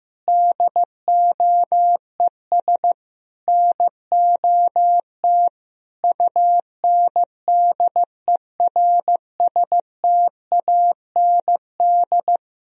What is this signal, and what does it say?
Morse code, heard as a keyed tone, about 15 wpm, 700 Hz. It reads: DOES NOT UNDERSTAND
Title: Sentences from Top 1000 Words - Review Part 2 - 15wpm